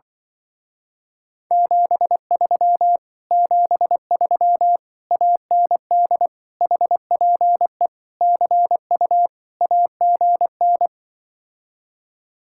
Morse code, heard as a keyed tone, about 24 wpm, 700 Hz. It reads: 73 73 AND HPE CU AGN
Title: Speed-Racing Common QSO Phrases Encore - 20wpm